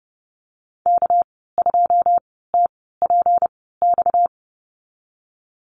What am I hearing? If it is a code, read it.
Morse code, heard as a keyed tone, about 30 wpm, spaced as for 17 wpm, 700 Hz. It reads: K2TPX
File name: Lesson 37g - Letter x - Call Signs 30wpm